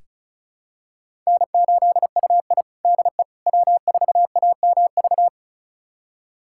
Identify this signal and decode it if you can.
Morse code, heard as a keyed tone, about 36 wpm, 700 Hz. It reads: N8UI DE W4AMV
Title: USA Callsign DE Callsign 35wpm